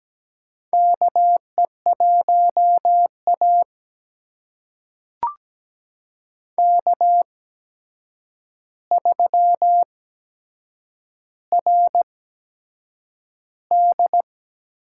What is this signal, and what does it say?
Morse code, heard as a keyed tone, about 17 wpm, spaced as for 7 wpm, 700 Hz. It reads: KE1A K3RD
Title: USA Call Signs - 4 Characters - 8x Character Spacing - 17wpm